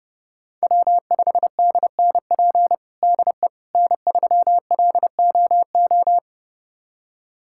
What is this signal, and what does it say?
Morse code, heard as a keyed tone, about 30 wpm, 700 Hz. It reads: W5DNP DE N3LOO
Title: USA Callsign DE Callsign - 5 Characters - 30wpm